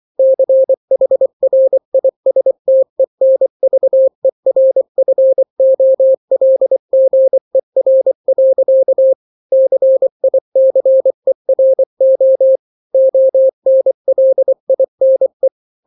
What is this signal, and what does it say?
Morse code, heard as a keyed tone, about 24 wpm, 530 Hz. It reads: CHRISTENVERFOLGER. CICERO ONLINE
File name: daily / 23wpm